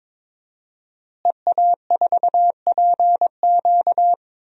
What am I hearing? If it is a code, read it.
Morse code, heard as a keyed tone, about 22 wpm, 700 Hz. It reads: EA4PQ
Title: DXCC-100 Call Signs - 5 Characters - Repeated 2x - 22wpm